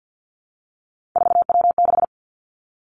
Morse code, heard as a keyed tone, about 50 wpm, 700 Hz. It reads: 4W6
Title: Three Challenging Letter-Number Combinations 50wpm